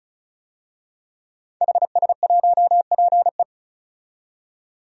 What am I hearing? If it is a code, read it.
Morse code, heard as a keyed tone, about 34 wpm, 700 Hz. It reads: HS1PE
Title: DXCC-100 Call Signs - 5 Characters - 35wpm